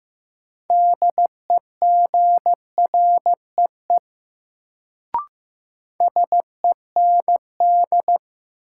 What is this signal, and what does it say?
Morse code, heard as a keyed tone, about 15 wpm, 700 Hz. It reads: DEGREE SEND
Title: Sentences from Top 1500 Words - New Word Drill - 15wpm